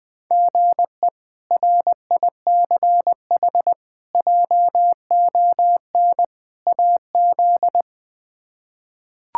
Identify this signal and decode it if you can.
Morse code, heard as a keyed tone, about 21 wpm, 700 Hz. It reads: GE RICH JON AZ
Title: K1USN SST QSO Practice - 20wpm